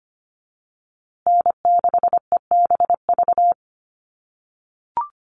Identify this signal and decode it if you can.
Morse code, heard as a keyed tone, about 25 wpm, 700 Hz. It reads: N6EBV
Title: USA Call Signs Repeated 3x 25wpm